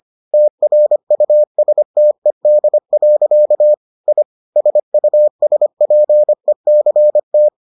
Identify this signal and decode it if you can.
Morse code, heard as a keyed tone, about 25 wpm, 600 Hz. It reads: TRUSTED. I SUSPECT